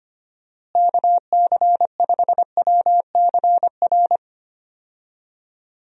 Morse code, heard as a keyed tone, about 25 wpm, 700 Hz. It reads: KC5WCR
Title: USA Call Signs - 6 Characters - Repeated 2x - 25wpm